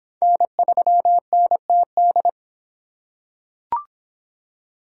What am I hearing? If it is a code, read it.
Morse code, heard as a keyed tone, about 22 wpm, 700 Hz. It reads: N3NTD